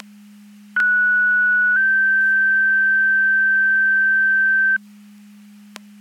Two-tone page, heard as a one-second tone, then a three-second tone, about 1470 then 1550 Hz. It is heard from 0.8 to 4.8 s.
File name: Portage 154.385 Paging